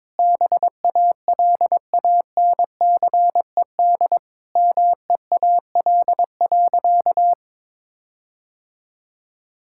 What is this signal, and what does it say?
Morse code, heard as a keyed tone, about 22 wpm, 700 Hz. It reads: BALANCED MEAL.
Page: morse-code-ninja-practice-22wpm